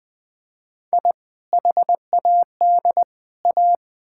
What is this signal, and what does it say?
Morse code, heard as a keyed tone, about 20 wpm, 700 Hz. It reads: I HAD A
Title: Build A Phrase - 3-Word Phrases - 20wpm